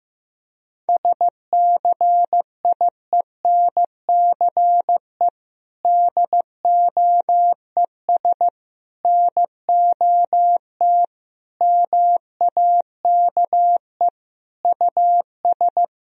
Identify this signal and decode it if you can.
Morse code, heard as a keyed tone, about 16 wpm, 700 Hz. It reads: SCIENCE DOES NOT MAKE US
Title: Sentences from Top 1000 Words - Review Part 2 - 15wpm